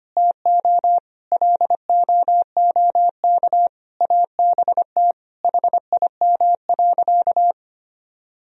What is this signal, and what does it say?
Morse code, heard as a keyed tone, about 25 wpm, 700 Hz. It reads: TO LOOK ABT HIM.
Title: Sentences from Top 200 Words and Abbreviations 25wpm